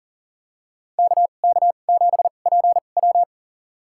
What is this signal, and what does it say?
Morse code, heard as a keyed tone, about 40 wpm, spaced as for 29 wpm, 700 Hz. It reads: KK7PW
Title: USA Call Signs - 5 Characters - 2x Character Spacing - 40wpm